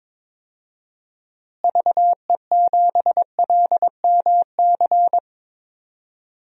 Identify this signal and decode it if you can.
Morse code, heard as a keyed tone, about 22 wpm, 700 Hz. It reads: VE7LMC
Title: DXCC-100 Call Signs - 6 Characters - Repeated 3x - 22wpm